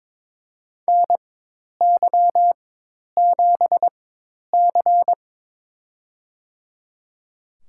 Morse code, heard as a keyed tone, about 22 wpm, spaced as for 10 wpm, 700 Hz. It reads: NY7C
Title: USA Call Signs - 4 Characters - 4x Character Spacing - 22wpm